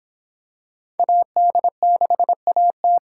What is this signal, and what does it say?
Morse code, heard as a keyed tone, about 26 wpm, 700 Hz. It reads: AD6AT